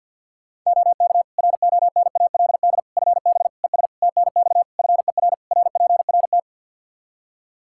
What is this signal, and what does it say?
Morse code, heard as a keyed tone, about 49 wpm, 700 Hz. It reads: OK RONALD FB ES TNX FER RPRT
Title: Practice QSO - Exchange 2 - Info - 50wpm